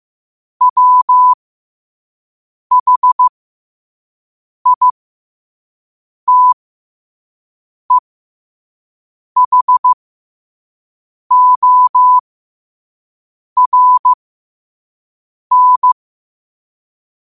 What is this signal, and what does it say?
Morse code, heard as a keyed tone, about 15 wpm, spaced as for 5 wpm, 1000 Hz. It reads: WHITEHORN
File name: qotdfiles / 5wpm